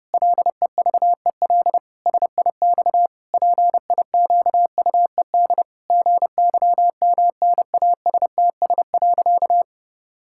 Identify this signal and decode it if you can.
Morse code, heard as a keyed tone, about 30 wpm, 700 Hz. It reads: LEVEL SIX PIQUED GYMNASTS.
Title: Pangram Challenge v2 - 30wpm